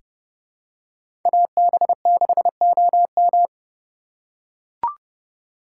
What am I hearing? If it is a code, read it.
Morse code, heard as a keyed tone, about 30 wpm, 700 Hz. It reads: AB6OM
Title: USA Call Signs - 6x Character Spacing - 30wpm